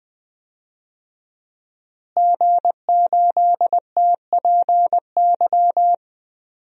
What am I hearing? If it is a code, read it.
Morse code, heard as a keyed tone, about 20 wpm, 700 Hz. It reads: G8TPY
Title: DXCC-100 Call Signs - 5 Characters - Repeated 3x - 20wpm